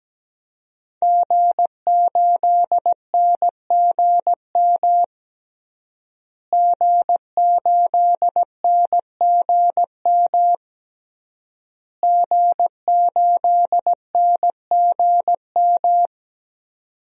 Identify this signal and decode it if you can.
Morse code, heard as a keyed tone, about 17 wpm, 700 Hz. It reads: G8NGM G8NGM G8NGM
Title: DXCC-100 Call Signs - 5 Characters - Repeated 3x - 17wpm